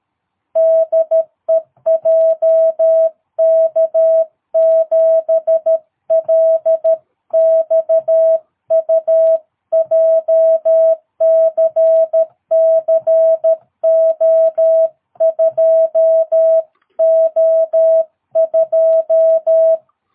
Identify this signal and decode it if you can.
Morse code, heard as a keyed tone, about 13 wpm, 660 Hz. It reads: DE JK7LXU JCC O2O2
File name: jh8cbh / 2022cw